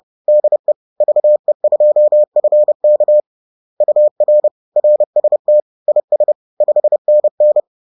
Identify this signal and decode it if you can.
Morse code, heard as a keyed tone, about 30 wpm, 600 Hz. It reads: DE VE2FK UR RST IS 5NN
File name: 30wpm-4